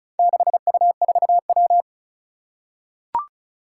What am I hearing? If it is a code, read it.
Morse code, heard as a keyed tone, about 36 wpm, 700 Hz. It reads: 6U4W
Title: Four Challenging Letter-Number Combinations 35wpm